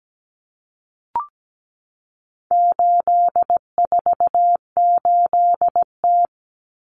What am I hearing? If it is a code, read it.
Morse code, heard as a keyed tone, about 17 wpm, 700 Hz. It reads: 848T
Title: Common Cut Numbers - 4-Digit 17wpm